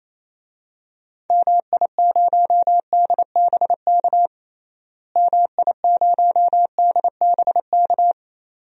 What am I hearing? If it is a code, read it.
Morse code, heard as a keyed tone, about 27 wpm, 700 Hz. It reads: MI0DBK MI0DBK
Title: DXCC-100 Call Signs - 6 Characters - Repeated 2x - 28wpm